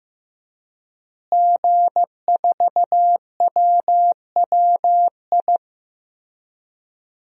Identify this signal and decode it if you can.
Morse code, heard as a keyed tone, about 15 wpm, 700 Hz. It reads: G4WWI
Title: DXCC-100 Call Signs - 5 Characters - Repeated 3x - 15wpm